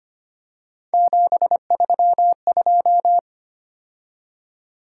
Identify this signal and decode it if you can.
Morse code, heard as a keyed tone, about 25 wpm, 700 Hz. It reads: 732
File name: Three Letter-Number Combinations - 25wpm